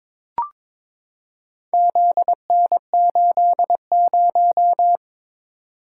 Morse code, heard as a keyed tone, about 22 wpm, 700 Hz. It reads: ZN80
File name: practice / 22wpm / Four Letter-Number Combinations - 22wpm